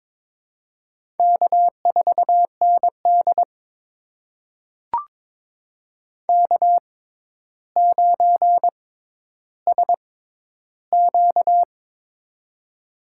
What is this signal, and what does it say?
Morse code, heard as a keyed tone, about 22 wpm, spaced as for 11 wpm, 700 Hz. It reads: K4ND K9SQ
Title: USA Call Signs - 4 Characters - 6x Character Spacing - 22wpm